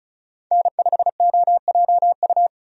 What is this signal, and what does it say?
Morse code, heard as a keyed tone, about 36 wpm, 700 Hz. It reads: N5OJU
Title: USA Call Signs - 5 Characters - Repeated 2x - 35wpm